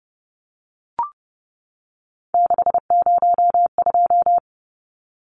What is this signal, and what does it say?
Morse code, heard as a keyed tone, about 30 wpm, 700 Hz. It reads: 602
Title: Numbers - 3-Digit 30wpm